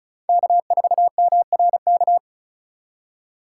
Morse code, heard as a keyed tone, about 34 wpm, 700 Hz. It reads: K4MRK
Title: USA Call Signs - 5 Characters - 3x Character Spacing - 35wpm